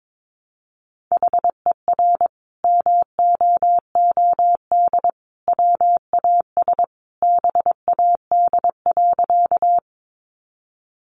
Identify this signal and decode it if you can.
Morse code, heard as a keyed tone, about 22 wpm, 700 Hz. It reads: HER MOOD WAS BAD.
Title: Sentences from Top 1100 Words 22wpm